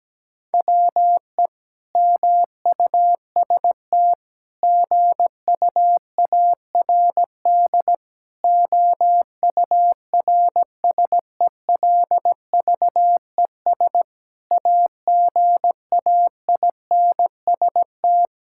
Morse code, heard as a keyed tone, about 17 wpm, 700 Hz. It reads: WE MUST GUARD OURSELVES AGAINST
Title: Sentences from Top 1600 Words 17wpm